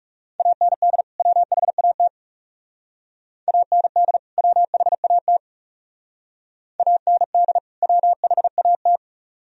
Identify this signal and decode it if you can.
Morse code, heard as a keyed tone, about 38 wpm, 700 Hz. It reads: AND WHAT AND WHAT AND WHAT